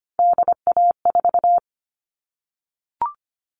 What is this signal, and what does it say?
Morse code, heard as a keyed tone, about 25 wpm, 700 Hz. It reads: DA4
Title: All Cut Numbers - 3-Digit 25wpm